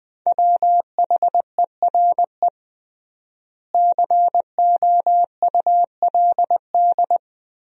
Morse code, heard as a keyed tone, about 20 wpm, 700 Hz. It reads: WHERE COULD
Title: 3x Word-Spacing - Sentences from Top 300 Words - 20wpm